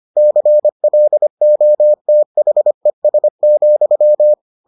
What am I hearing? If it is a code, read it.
Morse code, heard as a keyed tone, about 25 wpm, 600 Hz. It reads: CLOTHES,